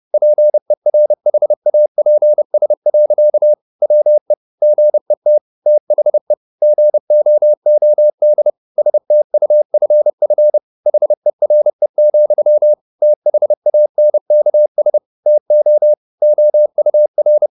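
Morse code, heard as a keyed tone, about 30 wpm, 600 Hz. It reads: PERHAPS. WE GET THE GOOD STUFF HERE, THANKS TO OUR
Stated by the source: Morse code at 30 wpm, the novel Never by Ken Follett in English, from CW med Gnister.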